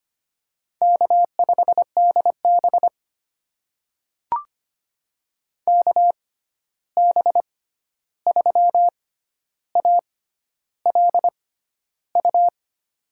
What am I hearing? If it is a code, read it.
Morse code, heard as a keyed tone, about 25 wpm, spaced as for 11 wpm, 700 Hz. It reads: K5DB KB3ALU
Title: USA Call Signs - 6x Character Spacing - 25wpm